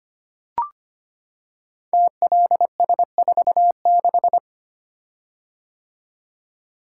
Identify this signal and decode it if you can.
Morse code, heard as a keyed tone, about 25 wpm, 700 Hz. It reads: TLS46
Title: Code Groups - 5 Letter:Numbers - 25wpm